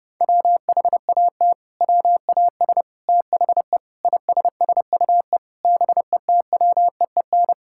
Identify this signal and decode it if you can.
Morse code, heard as a keyed tone, about 30 wpm, 700 Hz. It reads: WHAT WAS THE ISSUE BETWEEN
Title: Sentences from Top 1000 Words - Review Part 6 - 30wpm